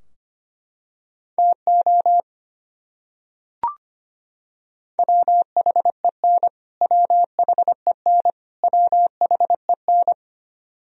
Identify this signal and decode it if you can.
Morse code, heard as a keyed tone, about 25 wpm, 700 Hz. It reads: TO WHEN WHEN WHEN
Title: Top 100 Words Repeated 25wpm